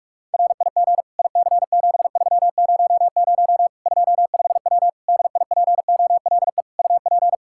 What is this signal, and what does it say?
Morse code, heard as a keyed tone, about 46 wpm, 700 Hz. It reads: RIG IC7300 25W DIPOLE UP
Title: K1USN SST QSO Practice 45wpm